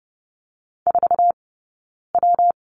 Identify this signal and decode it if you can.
Morse code, heard as a keyed tone, about 30 wpm, 700 Hz. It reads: 4 W
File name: Lesson 19d - Letter w - Two Characters 30wpm